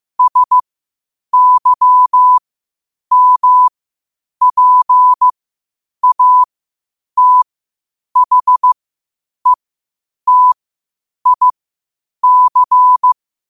Morse code, heard as a keyed tone, about 15 wpm, spaced as for 8 wpm, 1000 Hz. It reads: SYMPATHETIC
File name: qotdfiles / 8wpm